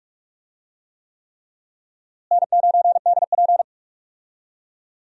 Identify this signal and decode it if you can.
Morse code, heard as a keyed tone, about 47 wpm, 700 Hz. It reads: N9DP
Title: USA Call Signs - 4 Characters - Repeated 3x - 45wpm